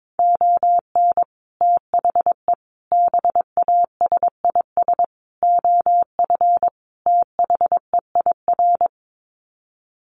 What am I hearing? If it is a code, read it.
Morse code, heard as a keyed tone, about 22 wpm, 700 Hz. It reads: ON THE BASIS OF THEIR
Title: Sets of 5 Words Encore 22wpm